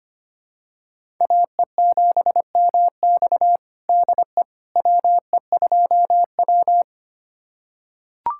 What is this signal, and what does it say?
Morse code, heard as a keyed tone, about 25 wpm, 700 Hz. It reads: AE7MX DE WE2W
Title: USA Callsign DE Callsign 25wpm